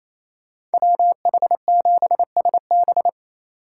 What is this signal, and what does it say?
Morse code, heard as a keyed tone, about 33 wpm, 700 Hz. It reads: WH7SB
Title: Speed Racing - USA 2x2 Call Signs 28wpm